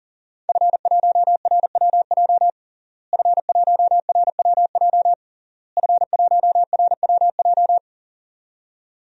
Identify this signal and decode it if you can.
Morse code, heard as a keyed tone, about 40 wpm, 700 Hz. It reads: F1RWJ F1RWJ F1RWJ
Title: DXCC-100 Call Signs - 5 Characters - Repeated 3x - 40wpm